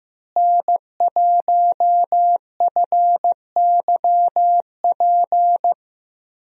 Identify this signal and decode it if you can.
Morse code, heard as a keyed tone, about 15 wpm, 700 Hz. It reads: N1FYP
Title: USA Call Signs - 5 Characters - Repeated 2x - 15wpm